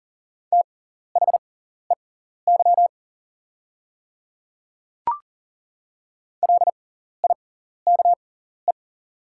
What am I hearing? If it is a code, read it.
Morse code, heard as a keyed tone, about 40 wpm, spaced as for 14 wpm, 700 Hz. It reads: THEY LIKE
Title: 6x Character Spacing - Top 100 Words - 40wpm